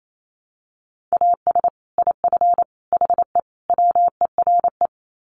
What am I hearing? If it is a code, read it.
Morse code, heard as a keyed tone, about 28 wpm, 700 Hz. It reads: AS IF HE WERE
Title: Sets of 4 Words 28wpm